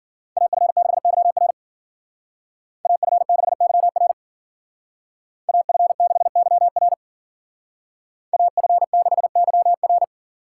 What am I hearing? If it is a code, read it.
Morse code, heard as a keyed tone, about 48 wpm, 700 Hz. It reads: AF6YR AF6YR AF6YR AF6YR
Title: Speed Racing - USA Call Signs 40wpm